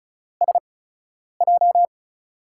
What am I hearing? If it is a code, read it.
Morse code, heard as a keyed tone, about 34 wpm, spaced as for 9 wpm, 700 Hz. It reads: SJ
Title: USA Call Signs - 4 Characters - 8x Character Spacing - 35wpm